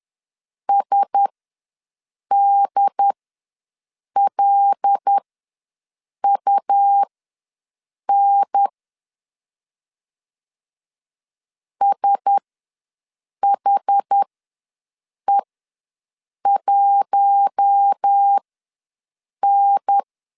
Morse code, heard as a keyed tone, about 10 wpm, spaced as for 6 wpm, 800 Hz. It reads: SDLUN SHE1N
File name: LEZIONI / 5 LBVC / 5 LB ws 10wpm 3616